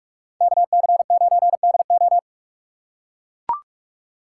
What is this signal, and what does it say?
Morse code, heard as a keyed tone, about 41 wpm, 700 Hz. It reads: KC9DO